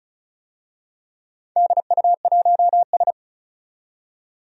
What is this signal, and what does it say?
Morse code, heard as a keyed tone, about 36 wpm, 700 Hz. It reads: DU1S
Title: DXCC-100 Call Signs - 4 Characters - 35wpm